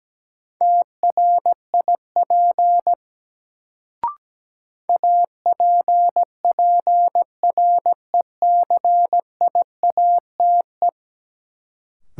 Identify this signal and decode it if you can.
Morse code, heard as a keyed tone, about 17 wpm, 700 Hz. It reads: TRIP APPRECIATE
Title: Sentences from Top 1300 Words - New Word Drill Alt - 17wpm